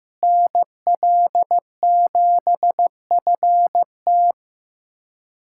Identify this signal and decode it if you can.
Morse code, heard as a keyed tone, about 15 wpm, 700 Hz. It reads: NL7FT
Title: USA Callsign DE Callsign - 5 Characters - 15wpm